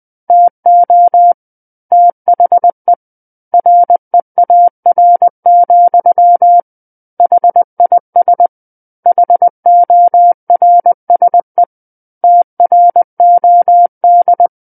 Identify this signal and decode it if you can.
Morse code, heard as a keyed tone, about 20 wpm, 700 Hz. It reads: TO THE REAR, HIS HORSE TROD
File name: 20 WPM